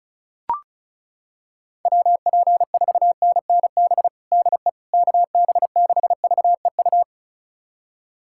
Morse code, heard as a keyed tone, about 34 wpm, 700 Hz. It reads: WP4NNB DE KB6VEU
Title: USA Callsign DE Callsign - 6 Characters - 35wpm